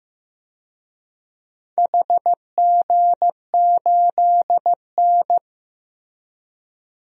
Morse code, heard as a keyed tone, about 15 wpm, 700 Hz. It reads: HG8N
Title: DXCC-100 Call Signs - 4 Characters - Repeated 2x - 15wpm